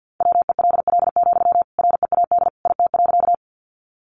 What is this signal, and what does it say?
Morse code, heard as a keyed tone, about 50 wpm, 700 Hz. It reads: WELL, READ IT.
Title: Sentences from Top 200 Words 50wpm